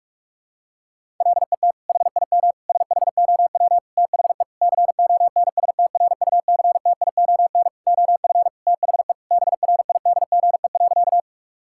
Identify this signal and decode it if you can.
Morse code, heard as a keyed tone, about 45 wpm, 700 Hz. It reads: LET HIM SHOW THE CONSTRUCTION OF THE BRIDGE.
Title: Sentences from Top 1600 Words 45wpm